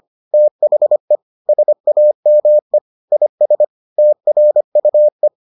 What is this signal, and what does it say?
Morse code, heard as a keyed tone, about 25 wpm, 600 Hz. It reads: THE SAME IS TRUE